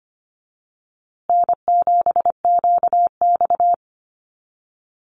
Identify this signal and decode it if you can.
Morse code, heard as a keyed tone, about 25 wpm, 700 Hz. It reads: N7QX
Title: USA Call Signs Repeated 3x 25wpm